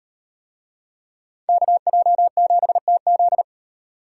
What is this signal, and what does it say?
Morse code, heard as a keyed tone, about 38 wpm, 700 Hz. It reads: KJ7TZ